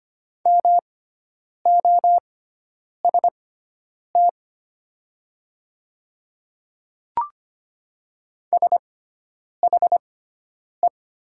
Morse code, heard as a keyed tone, about 25 wpm, spaced as for 9 wpm, 700 Hz. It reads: MOST SHE
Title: 6x Character Spacing - Top 100 Words - 25wpm